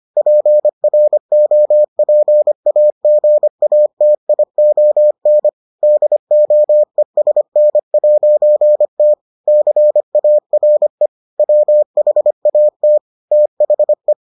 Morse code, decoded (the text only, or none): PROPAGATION DOESN'T CARE WHAT THE